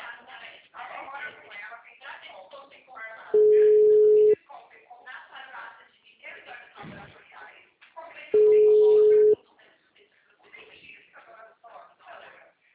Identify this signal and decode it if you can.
Ringback tone (Europe, 425 Hz), with speech.
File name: out- -2007- - - .69042